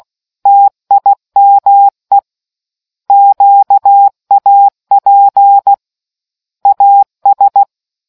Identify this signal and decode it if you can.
Morse code, heard as a keyed tone, about 16 wpm, 800 Hz. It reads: TIME QAP AS